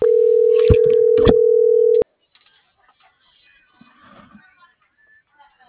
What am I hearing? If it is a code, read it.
Ringback tone (North America), 440 + 480 Hz.